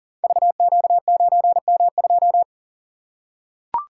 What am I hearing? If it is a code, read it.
Morse code, heard as a keyed tone, about 40 wpm, 700 Hz. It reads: VQ9M2
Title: Code Groups - 5 Letter:Numbers - 40wpm